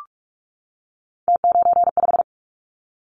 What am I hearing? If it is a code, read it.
Morse code, heard as a keyed tone, about 46 wpm, 700 Hz. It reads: T95
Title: Callsign Prefixes for Top 50 DXCC - 45wpm